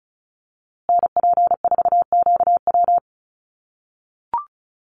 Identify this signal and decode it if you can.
Morse code, heard as a keyed tone, about 34 wpm, 700 Hz. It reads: NP4QW
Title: USA Call Signs Repeated 3x 35wpm